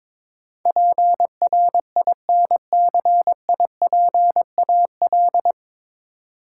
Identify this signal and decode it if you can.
Morse code, heard as a keyed tone, about 22 wpm, 700 Hz. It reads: PRINCIPAL
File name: 9 Letter Words - First 3 Letters Given - 22wpm